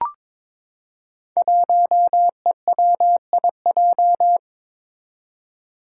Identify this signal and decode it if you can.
Morse code, heard as a keyed tone, about 21 wpm, 700 Hz. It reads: 1EWIJ
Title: Five Challenging Letter-Number Combinations 22wpm